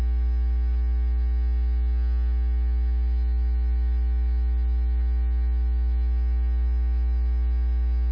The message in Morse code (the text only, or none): none